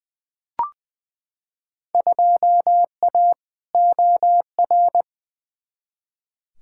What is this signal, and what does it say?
Morse code, heard as a keyed tone, about 20 wpm, 700 Hz. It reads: 2A OR
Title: Field Day Exchanges - 20wpm